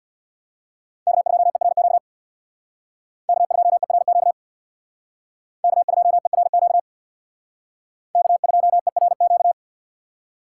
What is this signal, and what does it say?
Morse code, heard as a keyed tone, about 63 wpm, 700 Hz. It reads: K2ERQ K2ERQ K2ERQ K2ERQ